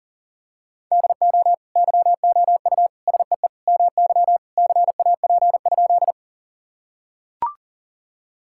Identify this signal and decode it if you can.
Morse code, heard as a keyed tone, about 40 wpm, 700 Hz. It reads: DO YOU SEE MY CAP?
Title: Sentences from Top 1600 Words 40wpm